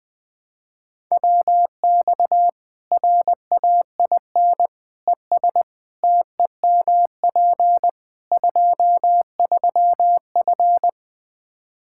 Morse code, heard as a keyed tone, about 20 wpm, 700 Hz. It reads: WX RAIN ES TEMP 23F
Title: K1USN SST QSO Practice 20wpm